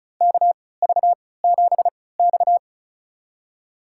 Morse code, heard as a keyed tone, about 36 wpm, spaced as for 20 wpm, 700 Hz. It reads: KV7X
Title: USA Call Signs - 4 Characters - 3x Character Spacing - 35wpm